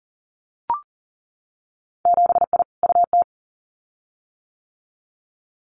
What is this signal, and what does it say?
Morse code, heard as a keyed tone, about 40 wpm, 700 Hz. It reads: 7I UT
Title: Winter Field Day Exchanges Encore 40wpm